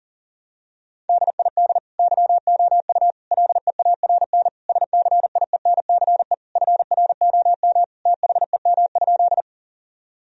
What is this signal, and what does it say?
Morse code, heard as a keyed tone, about 40 wpm, 700 Hz. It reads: DID YOU LEARN SCIENCE FROM THEM?
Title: Sentences from Top 1000 Words - Review Part 2 - 40wpm